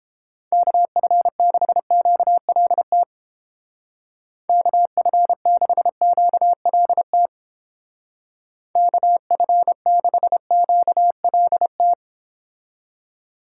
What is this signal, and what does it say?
Morse code, heard as a keyed tone, about 31 wpm, 700 Hz. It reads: KF6QLT KF6QLT KF6QLT